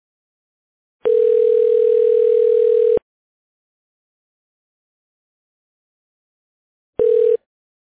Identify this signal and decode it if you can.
Ringback tone (North America), 440 + 480 Hz.